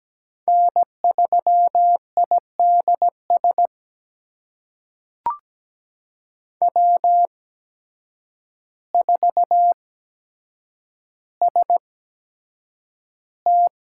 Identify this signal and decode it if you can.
Morse code, heard as a keyed tone, about 17 wpm, spaced as for 8 wpm, 700 Hz. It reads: N3IDS W4ST
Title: USA Call Signs - 5 Characters - 8x Character Spacing - 17wpm